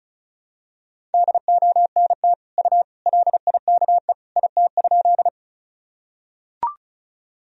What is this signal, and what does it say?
Morse code, heard as a keyed tone, about 36 wpm, 700 Hz. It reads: DONT U LIKE IT?
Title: Sentences from Top 300 Words and Abbreviations 35wpm